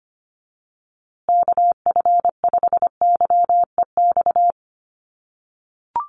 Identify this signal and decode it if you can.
Morse code, heard as a keyed tone, about 25 wpm, 700 Hz. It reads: KF5YEX
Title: USA Call Signs Repeated 3x 25wpm